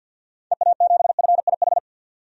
Morse code, heard as a keyed tone, about 52 wpm, 700 Hz. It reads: EA7FIH
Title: DXCC-100 Call Signs - 6 Characters - 50wpm